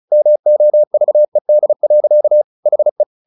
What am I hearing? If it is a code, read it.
Morse code, heard as a keyed tone, about 36 wpm, 600 Hz. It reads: MOVED. HE
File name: the old man and the sea / the old man and the sea 1 35wpm